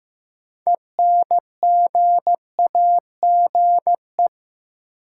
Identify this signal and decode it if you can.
Morse code, heard as a keyed tone, about 15 wpm, 700 Hz. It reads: ENGAGE
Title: Sentences from Top 1500 Words - New Word Drill Alt - 15wpm